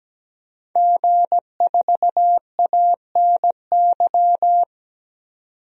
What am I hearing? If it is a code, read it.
Morse code, heard as a keyed tone, about 17 wpm, 700 Hz. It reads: G4ANY
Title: DXCC-100 Call Signs - 5 Characters - 17wpm